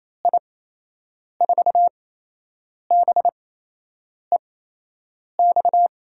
Morse code, heard as a keyed tone, about 27 wpm, 700 Hz. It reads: I 4 B E X
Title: USA Call Signs - 6 Characters - 8x Character Spacing - 28wpm